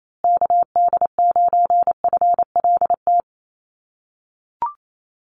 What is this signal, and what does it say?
Morse code, heard as a keyed tone, about 28 wpm, 700 Hz. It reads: KD9FLT
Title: USA Call Signs Repeated 3x 28wpm